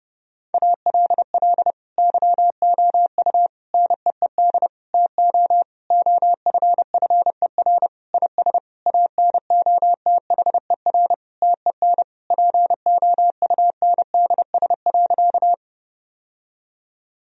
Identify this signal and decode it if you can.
Morse code, heard as a keyed tone, about 30 wpm, 700 Hz. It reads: ALL YOU NEED TO OFFER IS ANOTHER TEN POUNDS.
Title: Sentences from Top 1000 Words - Review Part 4 - 30wpm